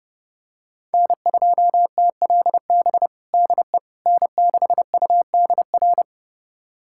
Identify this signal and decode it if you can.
Morse code, heard as a keyed tone, about 30 wpm, 700 Hz. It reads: N2TLB DE N6UDR
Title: USA Callsign DE Callsign - 5 Characters - 30wpm